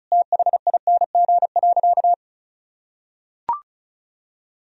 Morse code, heard as a keyed tone, about 35 wpm, 700 Hz. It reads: THING.
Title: Sentences from Top 1000 Words - Review Part 1 - 35wpm